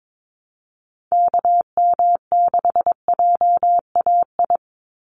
Morse code, heard as a keyed tone, about 22 wpm, 700 Hz. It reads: KM6JAI
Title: USA Call Signs 22wpm